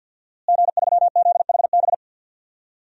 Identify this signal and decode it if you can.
Morse code, heard as a keyed tone, about 51 wpm, 700 Hz. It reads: G3ZHB
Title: DXCC-100 Call Signs - 5 Characters - Repeated 3x - 50wpm